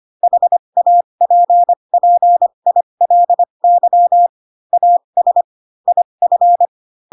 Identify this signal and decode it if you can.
Morse code, heard as a keyed tone, about 25 wpm, spaced as for 22 wpm, 700 Hz. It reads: HAPPILY AS IF
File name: Pinocchio / Pinocchio ch6 18 25 ik1hge